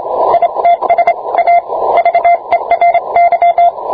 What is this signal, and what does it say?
Morse code, heard as a keyed tone, about 26 wpm, 690 Hz. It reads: ITS A VERY